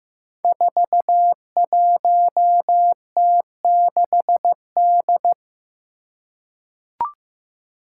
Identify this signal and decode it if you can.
Morse code, heard as a keyed tone, about 15 wpm, 700 Hz. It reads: 41T6D
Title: Five Challenging Letter-Number Combinations 15wpm